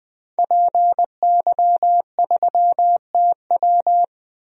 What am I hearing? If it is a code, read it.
Morse code, heard as a keyed tone, about 20 wpm, 700 Hz. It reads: PY3TW
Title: DXCC-100 Call Signs - 5 Characters - 20wpm